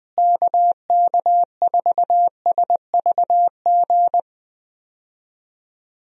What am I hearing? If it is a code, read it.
Morse code, heard as a keyed tone, about 20 wpm, 700 Hz. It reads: KK4SVG
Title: USA Call Signs - 6 Characters - Repeated 3x - 20wpm